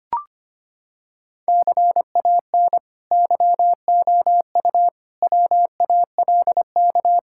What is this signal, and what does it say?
Morse code, heard as a keyed tone, about 25 wpm, 700 Hz. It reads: CAN YOU WALK
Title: Sentences from Top 1000 Words - Review Part 6 - 25wpm